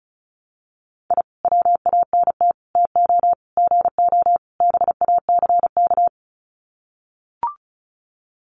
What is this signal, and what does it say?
Morse code, heard as a keyed tone, about 34 wpm, 700 Hz. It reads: I WANT TO GO BACK
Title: Sets of 5 Words Encore 35wpm